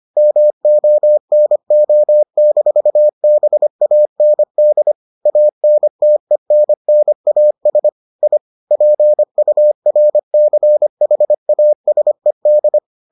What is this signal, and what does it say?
Morse code, heard as a keyed tone, about 25 wpm, 600 Hz. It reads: MONO-BAND ANTENNAS I PURCHASED